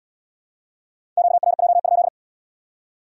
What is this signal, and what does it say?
Morse code, heard as a keyed tone, about 75 wpm, 700 Hz. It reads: QRJ?